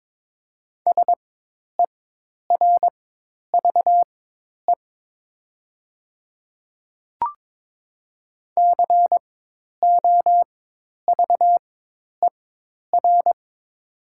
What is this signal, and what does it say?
Morse code, heard as a keyed tone, about 22 wpm, spaced as for 10 wpm, 700 Hz. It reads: SERVE COVER
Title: 4x Character Spacing - Top 500 Words - 22wpm